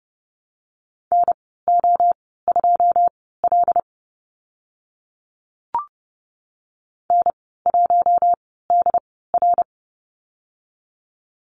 Morse code, heard as a keyed tone, about 30 wpm, spaced as for 17 wpm, 700 Hz. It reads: NO2L N1DR
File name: Lesson 14g - Number 2 - Call Signs 30wpm